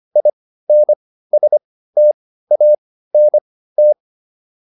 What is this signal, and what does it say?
Morse code, heard as a keyed tone, about 25 wpm, spaced as for 15 wpm, 600 Hz. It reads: INSTANT
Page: intermediate-practice-files